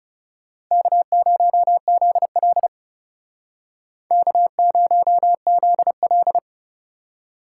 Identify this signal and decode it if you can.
Morse code, heard as a keyed tone, about 40 wpm, 700 Hz. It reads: K0ZL K0ZL